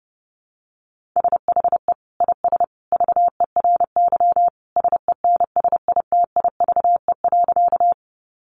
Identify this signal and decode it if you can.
Morse code, heard as a keyed tone, about 30 wpm, 700 Hz. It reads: SHE IS VERY SENSITIVE.
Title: Sentences from Top 1400 Words 30wpm